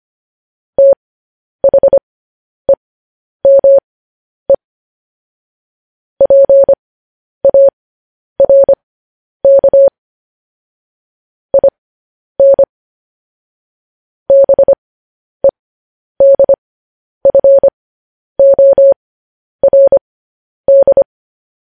Morse code, heard as a keyed tone, about 25 wpm, spaced as for 10 wpm, 550 Hz. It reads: THEME PARK IN BEDFORD
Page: news headlines in morse code at 10 wpm